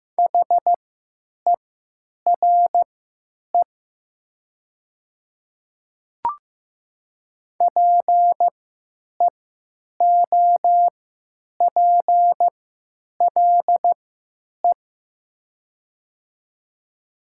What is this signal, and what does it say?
Morse code, heard as a keyed tone, about 15 wpm, spaced as for 9 wpm, 700 Hz. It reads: HERE PEOPLE
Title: 3x Character Spacing - Top 200 Words - 15wpm